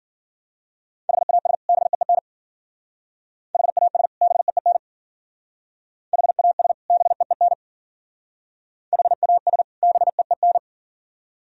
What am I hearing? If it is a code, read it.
Morse code, heard as a keyed tone, about 50 wpm, 690 Hz. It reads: HAS BEEN HAS BEEN HAS BEEN HAS BEEN
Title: Speed-Racing - Sets of 2 Words 40wpm